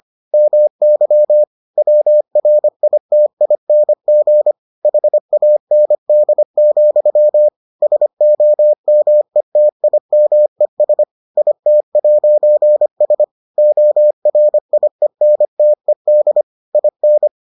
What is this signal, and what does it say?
Morse code, heard as a keyed tone, about 25 wpm, 600 Hz. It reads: MY WRITING HAND, SOMETIMES IT'S ORIENTED IN